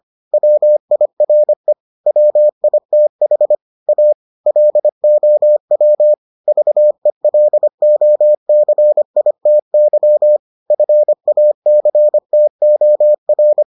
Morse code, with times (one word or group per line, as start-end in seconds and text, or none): 0.3-1.7 WIRE
2.1-3.5 WITH
3.9-4.1 A
4.5-6.1 LOW
6.5-10.4 VELOCITY
10.7-13.6 FACTOR